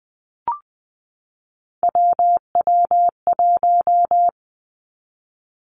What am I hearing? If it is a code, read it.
Morse code, heard as a keyed tone, about 20 wpm, 700 Hz. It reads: WW1
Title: Three Challenging Letter-Number Combinations 20wpm